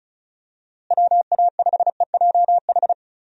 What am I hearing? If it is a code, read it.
Morse code, heard as a keyed tone, about 35 wpm, 700 Hz. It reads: WA5EJH
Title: USA Call Signs - 6 Characters - Repeated 2x - 35wpm